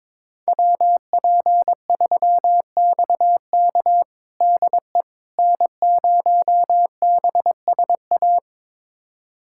Morse code, heard as a keyed tone, about 22 wpm, 700 Hz. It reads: WP3XK DE N0BSA
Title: USA Callsign DE Callsign - 5 Characters - 22wpm